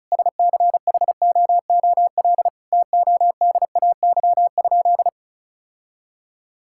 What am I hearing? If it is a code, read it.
Morse code, heard as a keyed tone, about 36 wpm, 700 Hz. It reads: SCHOOL TODAY?
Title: Sentences from Top 1000 Words - Review Part 5 - 35wpm